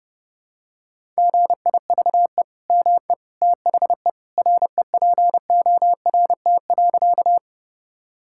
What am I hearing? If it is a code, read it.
Morse code, heard as a keyed tone, about 30 wpm, 700 Hz. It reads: GIVE ME THE REPORT.
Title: Sentences from Top 1000 Words - Review Part 6 - 30wpm